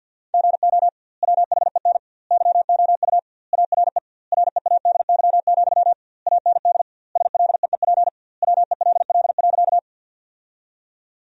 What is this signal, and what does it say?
Morse code, heard as a keyed tone, about 52 wpm, 700 Hz. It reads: GO WHEN YOU ARE READY, AND SLEEP WELL.
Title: 2x Word-Spacing - Sentences from Top 500 Words - 50wpm